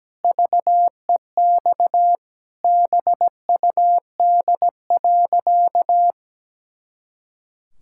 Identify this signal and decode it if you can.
Morse code, heard as a keyed tone, about 17 wpm, 700 Hz. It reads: VEX BUD.
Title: Pangram Challenge v2 - 17wpm